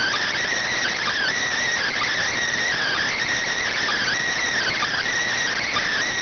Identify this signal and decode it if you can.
Emergency Alert System (SAME) tones (header bursts), heard as harsh buzzing bursts. It is heard from 0.7 to 5.1 s.